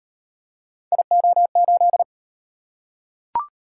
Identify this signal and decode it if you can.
Morse code, heard as a keyed tone, about 38 wpm, 700 Hz. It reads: IO8